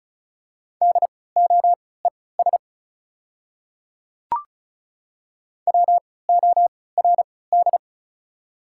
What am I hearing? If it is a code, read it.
Morse code, heard as a keyed tone, about 36 wpm, spaced as for 20 wpm, 690 Hz. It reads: DOES WORD
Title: 3x Character Spacing - Top 300 Words - 35wpm